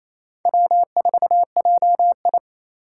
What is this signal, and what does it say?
Morse code, heard as a keyed tone, about 27 wpm, 700 Hz. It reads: W4JI
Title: USA Call Signs - 4 Characters - 2x Character Spacing - 28wpm